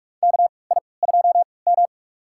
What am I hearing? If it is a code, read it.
Morse code, heard as a keyed tone, about 46 wpm, spaced as for 26 wpm, 700 Hz. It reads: KI2M
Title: USA Call Signs - 4 Characters - 3x Character Spacing - 45wpm